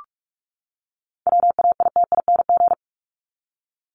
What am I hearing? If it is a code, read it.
Morse code, heard as a keyed tone, about 44 wpm, 700 Hz. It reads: WAITING
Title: ING Suffix - Top 100 Words with Base Words 45wpm